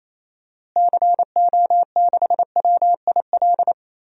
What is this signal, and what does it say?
Morse code, heard as a keyed tone, about 28 wpm, 700 Hz. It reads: CO6WIL
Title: DXCC-100 Call Signs - 6 Characters - Repeated 2x - 28wpm